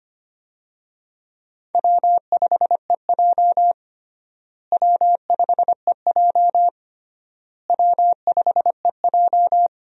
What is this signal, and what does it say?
Morse code, heard as a keyed tone, about 25 wpm, 700 Hz. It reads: W5EJ W5EJ W5EJ
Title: USA Call Signs - 4 Characters - Repeated 3x - 25wpm